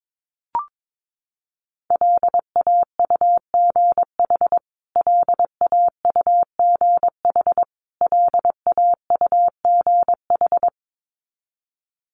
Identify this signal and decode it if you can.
Morse code, heard as a keyed tone, about 22 wpm, 700 Hz. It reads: LAUGH LAUGH LAUGH
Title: Top 500 Words Repeated 22wpm